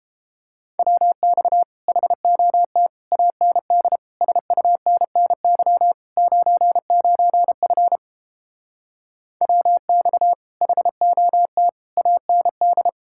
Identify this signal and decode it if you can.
Morse code, heard as a keyed tone, about 29 wpm, 700 Hz. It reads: WX HOT AND SUNNY 99F WX HOT AND